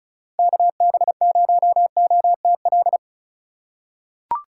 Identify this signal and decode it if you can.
Morse code, heard as a keyed tone, about 34 wpm, 700 Hz. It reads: KB0OTL
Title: USA Call Signs - 6 Characters - 4x Character Spacing - 35wpm